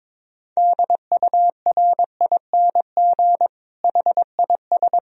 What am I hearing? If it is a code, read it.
Morse code, heard as a keyed tone, about 22 wpm, 700 Hz. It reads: DURING HIS
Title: Sentences from Top 1000 Words - Review Part 2 - 22wpm